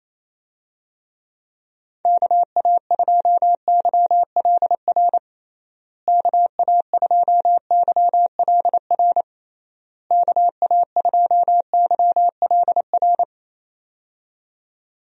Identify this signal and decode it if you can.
Morse code, heard as a keyed tone, about 28 wpm, 700 Hz. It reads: KA2YLR KA2YLR KA2YLR
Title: USA Call Signs - 6 Characters - Repeated 3x - 28wpm